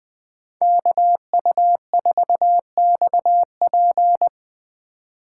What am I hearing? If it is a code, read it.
Morse code, heard as a keyed tone, about 23 wpm, 700 Hz. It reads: KU4XP
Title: Speed Racing - USA Call Signs 20wpm